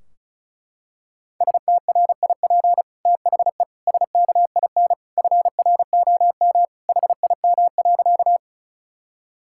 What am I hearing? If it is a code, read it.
Morse code, heard as a keyed tone, about 34 wpm, 700 Hz. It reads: STRIP THE SKIN FROM HIM.
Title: Sentences from Top 1600 Words 35wpm